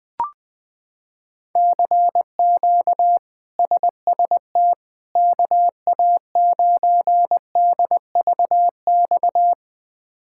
Morse code, heard as a keyed tone, about 20 wpm, 700 Hz. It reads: CQ SST KA9DVX
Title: K1USN SST QSO Practice - 20wpm